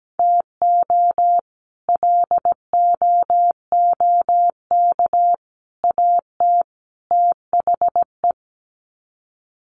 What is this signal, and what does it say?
Morse code, heard as a keyed tone, about 17 wpm, 700 Hz. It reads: TO LOOK AT THE
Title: Sets of 4 Words 17wpm